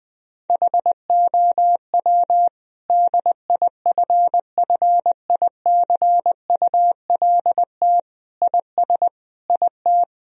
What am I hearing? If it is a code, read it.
Morse code, heard as a keyed tone, about 20 wpm, 700 Hz. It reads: HOW DIFFICULT IS IT
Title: Sentences from Top 1000 Words - Review Part 3 - 20wpm